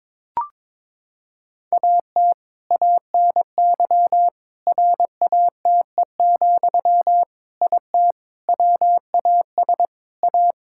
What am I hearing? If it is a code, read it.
Morse code, heard as a keyed tone, about 22 wpm, 700 Hz. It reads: AT ANY RATE, IT WAS A
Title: Sentences from Top 1600 Words 22wpm